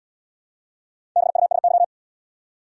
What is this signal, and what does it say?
Morse code, heard as a keyed tone, about 72 wpm, 700 Hz. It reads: BUSY